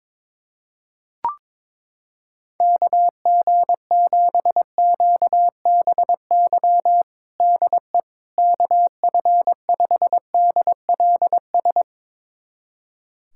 Morse code, heard as a keyed tone, about 22 wpm, 700 Hz. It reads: KG7QBY DE KF5DLS
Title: USA Callsign DE Callsign - 6 Characters - 22wpm